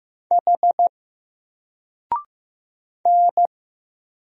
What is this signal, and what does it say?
Morse code, heard as a keyed tone, about 15 wpm, 700 Hz. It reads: H N
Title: Single Letter-Number 15wpm